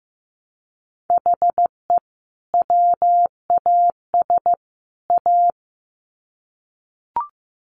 Morse code, heard as a keyed tone, about 15 wpm, 700 Hz. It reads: HE WAS A
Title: Sets of 3 Words 15wpm